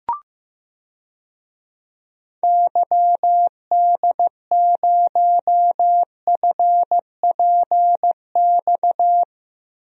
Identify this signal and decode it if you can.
Morse code, heard as a keyed tone, about 15 wpm, 700 Hz. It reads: YD0FPX
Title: DXCC-100 Call Signs - 6 Characters - Repeated 3x - 15wpm